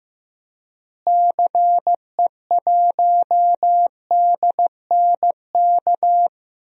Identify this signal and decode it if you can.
Morse code, heard as a keyed tone, about 15 wpm, 700 Hz. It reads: CE1DNK
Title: DXCC-100 Call Signs - 6 Characters - Repeated 3x - 15wpm